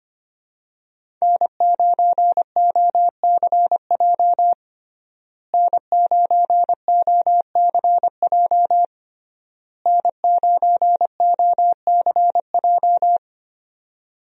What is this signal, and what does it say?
Morse code, heard as a keyed tone, about 25 wpm, 700 Hz. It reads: N9OCJ N9OCJ N9OCJ
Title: USA Call Signs - 5 Characters - Repeated 3x - 25wpm